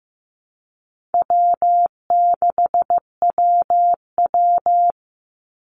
Morse code, heard as a keyed tone, about 15 wpm, 700 Hz. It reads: W6WW
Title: NA SOTA Chaser Callsigns 15wpm